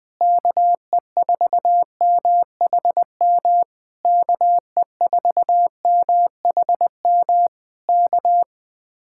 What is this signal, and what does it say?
Morse code, heard as a keyed tone, about 21 wpm, 700 Hz. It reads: KE4MHM KE4MHM K